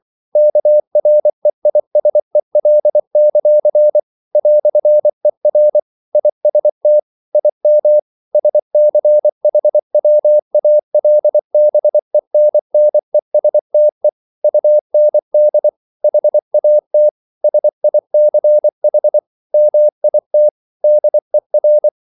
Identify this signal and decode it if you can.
Morse code, heard as a keyed tone, about 24 wpm, 600 Hz. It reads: KREISEL; "ER IST IM SCHWALBENNESTE UND HAT SICH MIT DER
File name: Liebespaar2 24wpm